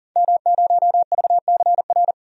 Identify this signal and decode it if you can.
Morse code, heard as a keyed tone, about 40 wpm, 700 Hz. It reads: M0VCR
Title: DXCC-100 Call Signs - 5 Characters - 40wpm